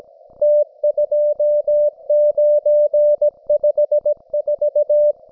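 Morse code, heard as a keyed tone, about 17 wpm, 590 Hz. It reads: T2954
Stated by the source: Russian military Morse station, M32, RDL, on 5268 kHz.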